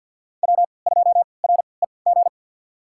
Morse code, heard as a keyed tone, about 51 wpm, spaced as for 28 wpm, 700 Hz. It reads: W2REG
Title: USA Call Signs - 5 Characters - 3x Character Spacing - 50wpm